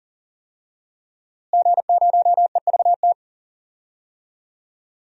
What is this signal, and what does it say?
Morse code, heard as a keyed tone, about 40 wpm, 700 Hz. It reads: G0EVT
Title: DXCC-100 Call Signs - 5 Characters - Repeated 3x - 40wpm